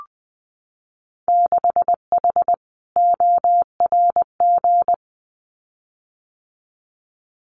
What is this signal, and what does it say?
Morse code, heard as a keyed tone, about 20 wpm, 700 Hz. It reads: 6H ORG
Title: Winter Field Day Exchanges Encore 20wpm